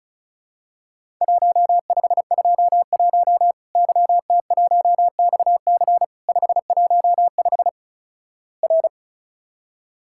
Morse code, heard as a keyed tone, about 35 wpm, 700 Hz. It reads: 1521 YT1XC 515 R
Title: Worked All Europe DX Contest - QTC Exchange - 35wpm